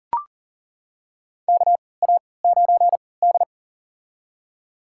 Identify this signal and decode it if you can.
Morse code, heard as a keyed tone, about 40 wpm, spaced as for 23 wpm, 700 Hz. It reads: KA9D
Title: USA Call Signs - 4 Characters - 3x Character Spacing - 40wpm